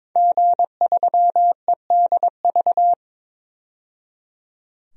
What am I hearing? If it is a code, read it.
Morse code, heard as a keyed tone, about 22 wpm, 700 Hz. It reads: G3EDV
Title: DXCC-100 Call Signs - 5 Characters - 22wpm